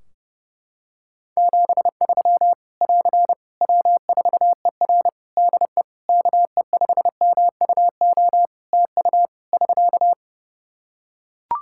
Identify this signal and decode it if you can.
Morse code, heard as a keyed tone, about 30 wpm, 700 Hz. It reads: 73 <AR> W4ER DE KE5MUO TU <SK>
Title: K1USN SST QSO Practice 30wpm